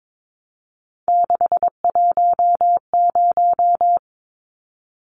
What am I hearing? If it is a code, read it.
Morse code, heard as a keyed tone, about 22 wpm, 700 Hz. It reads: 610
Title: Numbers - 3-Digit 22wpm